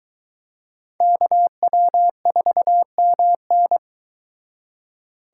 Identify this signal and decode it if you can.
Morse code, heard as a keyed tone, about 22 wpm, 700 Hz. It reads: KW4MN